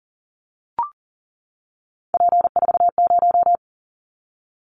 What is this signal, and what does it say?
Morse code, heard as a keyed tone, about 40 wpm, 700 Hz. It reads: P40
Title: Callsign Prefixes for Top 100 DXCC - 40wpm